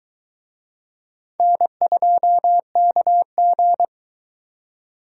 Morse code, heard as a keyed tone, about 23 wpm, 700 Hz. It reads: N2KG